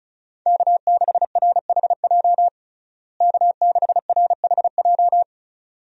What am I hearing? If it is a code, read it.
Morse code, heard as a keyed tone, about 34 wpm, 700 Hz. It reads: K6RHJ K6RHJ
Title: USA Call Signs - 5 Characters - Repeated 2x - 35wpm